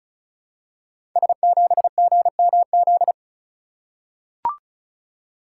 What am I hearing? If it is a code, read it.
Morse code, heard as a keyed tone, about 36 wpm, 700 Hz. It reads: S7GMZ
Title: Code Groups - 5 Letter:Numbers - 35wpm